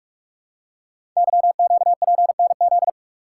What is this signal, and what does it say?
Morse code, heard as a keyed tone, about 44 wpm, 700 Hz. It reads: YQPNZ
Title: Code Groups - 5 Letters - 45wpm